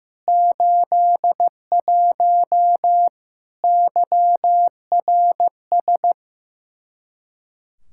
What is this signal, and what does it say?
Morse code, heard as a keyed tone, about 15 wpm, 700 Hz. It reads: 81 YRS
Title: Practice QSO - Exchange 3 - More Info - 15wpm